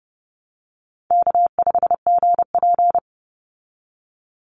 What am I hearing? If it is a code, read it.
Morse code, heard as a keyed tone, about 30 wpm, 700 Hz. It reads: K5GP
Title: Easy Call Signs Repeated 3x - 30wpm